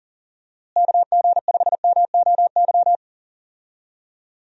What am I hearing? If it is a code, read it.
Morse code, heard as a keyed tone, about 40 wpm, 700 Hz. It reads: KG5MOY
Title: USA Call Signs - 6 Characters - Repeated 2x - 40wpm